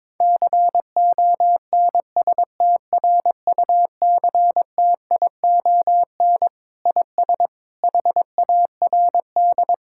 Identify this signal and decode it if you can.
Morse code, heard as a keyed tone, about 22 wpm, 700 Hz. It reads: CONSTRUCTION IS HARD
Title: Sentences from Top 1600 Words 22wpm